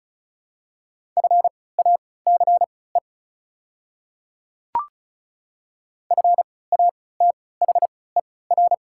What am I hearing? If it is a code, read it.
Morse code, heard as a keyed tone, about 36 wpm, spaced as for 20 wpm, 690 Hz. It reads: FACE FATHER
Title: 3x Character Spacing - Top 300 Words - 35wpm